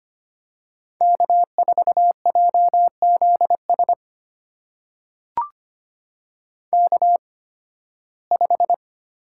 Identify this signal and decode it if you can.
Morse code, heard as a keyed tone, about 25 wpm, spaced as for 16 wpm, 700 Hz. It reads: K4JZS K5
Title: USA Call Signs - 5 Characters - 8x Character Spacing - 25wpm